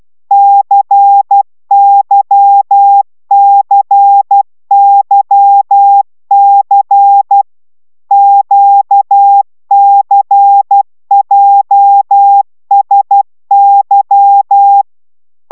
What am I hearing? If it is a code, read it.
Morse code, heard as a keyed tone, about 12 wpm, 800 Hz. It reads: CYCYC QCJSY